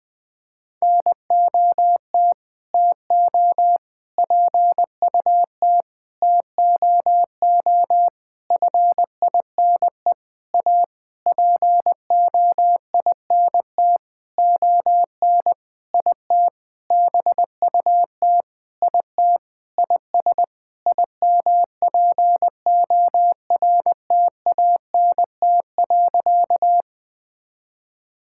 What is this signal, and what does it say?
Morse code, heard as a keyed tone, about 20 wpm, 700 Hz. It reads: NOT TO PUT TOO FINE A POINT ON IT BUT IT IS IMPORTANT.
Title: Sentences with 8-Word Phrases - 20wpm